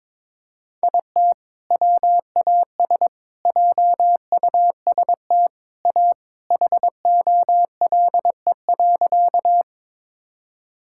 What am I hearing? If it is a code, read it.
Morse code, heard as a keyed tone, about 22 wpm, 700 Hz. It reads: IT WAS JUST A HOLE.
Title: Sentences from Top 1600 Words 22wpm